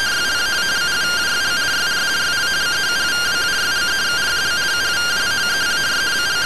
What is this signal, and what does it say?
Two-tone teleprinter signal, about 170 Hz shift.